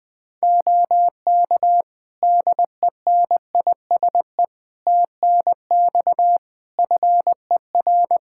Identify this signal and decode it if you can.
Morse code, heard as a keyed tone, about 19 wpm, 700 Hz. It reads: OK DENISE TNX FER
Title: K1USN SST QSO Practice 20wpm